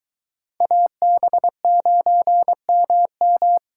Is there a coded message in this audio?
Yes: Morse code from 0.6 to 3.6 s, heard as a keyed tone, about 22 wpm, 700 Hz. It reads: AB9MM